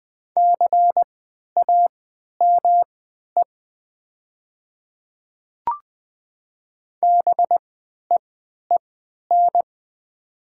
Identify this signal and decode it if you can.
Morse code, heard as a keyed tone, about 20 wpm, spaced as for 11 wpm, 700 Hz. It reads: CAME BEEN
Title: 3x Character Spacing - Top 200 Words - 20wpm